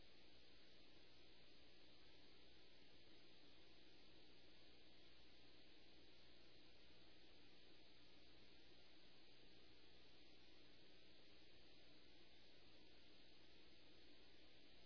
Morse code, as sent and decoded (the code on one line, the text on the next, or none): none